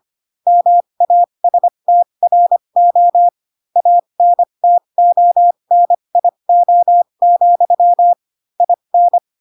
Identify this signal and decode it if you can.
Morse code, heard as a keyed tone, about 25 wpm, spaced as for 22 wpm, 700 Hz. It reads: MASTRO ANTONIO, IN
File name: Pinocchio / Pinocchio ch2 18 25 ik1hge